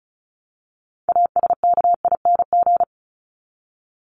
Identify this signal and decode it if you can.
Morse code, heard as a keyed tone, about 34 wpm, 700 Hz. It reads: ASKING
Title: ING Suffix - Top 100 Words with Base Words 35wpm